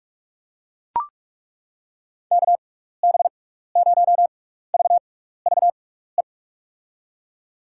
Morse code, heard as a keyed tone, about 42 wpm, spaced as for 15 wpm, 700 Hz. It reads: KB0VVE
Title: USA Call Signs - 6 Characters - 6x Character Spacing - 45wpm